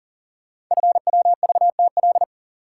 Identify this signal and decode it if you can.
Morse code, heard as a keyed tone, about 40 wpm, 700 Hz. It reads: FWVTL
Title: Code Groups - 5 Letter:Numbers - 40wpm